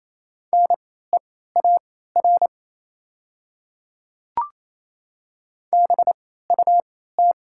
Morse code, heard as a keyed tone, about 28 wpm, spaced as for 16 wpm, 700 Hz. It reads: NEAR BUT
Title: 3x Character Spacing - Top 200 Words - 28wpm